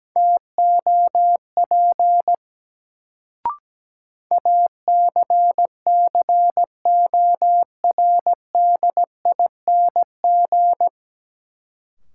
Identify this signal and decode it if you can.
Morse code, heard as a keyed tone, about 17 wpm, 700 Hz. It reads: TOP ACCORDING
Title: Sentences from Top 700 Words - New Word Drill - 17wpm